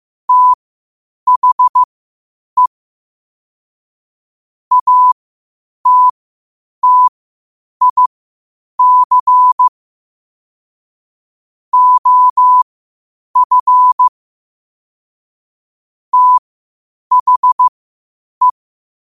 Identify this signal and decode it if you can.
Morse code, heard as a keyed tone, about 15 wpm, spaced as for 8 wpm, 1000 Hz. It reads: THE ATTIC OF THE
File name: qotdfiles / 8wpm